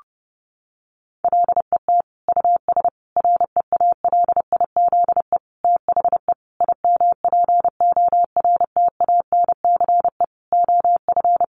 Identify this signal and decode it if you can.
Morse code, heard as a keyed tone, about 30 wpm, 700 Hz. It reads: LET US REALIZE THE IMPORTANCE OF
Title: Sentences from Top 1300 Words 30wpm